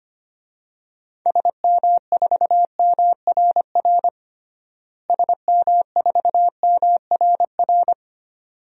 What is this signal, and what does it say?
Morse code, heard as a keyed tone, about 25 wpm, 700 Hz. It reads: SM4MRR SM4MRR
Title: DXCC-100 Call Signs - 6 Characters - Repeated 2x - 25wpm